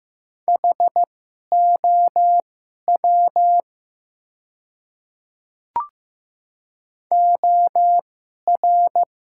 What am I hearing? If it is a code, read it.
Morse code, heard as a keyed tone, about 15 wpm, spaced as for 11 wpm, 700 Hz. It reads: HOW OR
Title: 2x Character Spacing - Top 100 Words - 15wpm